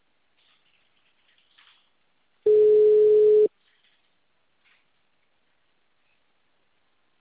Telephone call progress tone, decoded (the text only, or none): ringback tone